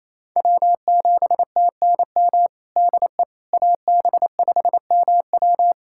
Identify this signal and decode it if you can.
Morse code, heard as a keyed tone, about 28 wpm, 690 Hz. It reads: W7TNM DE AB5MW
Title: USA Callsign DE Callsign - 5 Characters - 28wpm